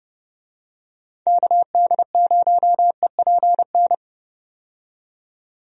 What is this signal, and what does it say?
Morse code, heard as a keyed tone, about 30 wpm, 700 Hz. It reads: KD0EPN
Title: USA Call Signs - 6 Characters - 30wpm